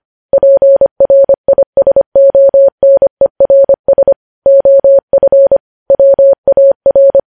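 Morse code, heard as a keyed tone, about 25 wpm, 550 Hz. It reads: PRISONERS OF WAR
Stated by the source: Morse code news headlines sent at 25.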